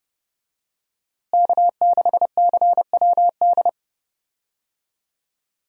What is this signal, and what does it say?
Morse code, heard as a keyed tone, about 32 wpm, 700 Hz. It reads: K6CWD
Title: USA Call Signs - 5 Characters - Repeated 2x - 30wpm